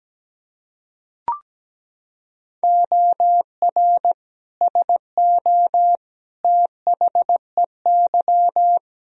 Morse code, heard as a keyed tone, about 17 wpm, 700 Hz. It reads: OR SO THEY
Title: Sentences from Top 100 Words and Abbreviations 17wpm